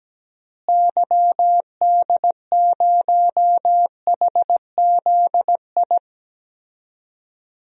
Morse code, heard as a keyed tone, about 17 wpm, 700 Hz. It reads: YD0HZI
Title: DXCC-100 Call Signs - 6 Characters - Repeated 3x - 17wpm